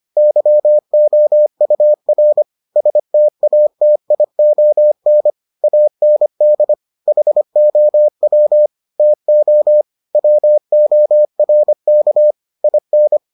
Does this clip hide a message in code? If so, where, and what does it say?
Yes: Morse code from 0.2 to 13.2 s, heard as a keyed tone, about 25 wpm, 600 Hz. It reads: YOUR STATION AND HOW TO WORK IN